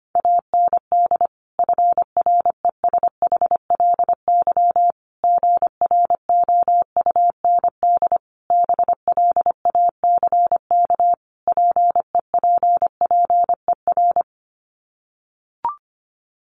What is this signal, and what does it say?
Morse code, heard as a keyed tone, about 25 wpm, 700 Hz. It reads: AND FRESHLY GROUND BLACK PEPPER
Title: Sets of 5 Words Encore 25wpm